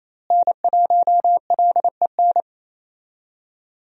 Morse code, heard as a keyed tone, about 28 wpm, 700 Hz. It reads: N1LEN
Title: USA Call Signs - 5 Characters - Repeated 3x - 28wpm